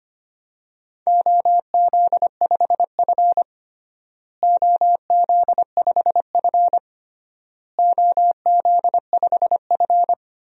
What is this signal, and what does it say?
Morse code, heard as a keyed tone, about 25 wpm, 700 Hz. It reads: OZ5F OZ5F OZ5F
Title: DXCC-100 Call Signs - 4 Characters - Repeated 3x - 25wpm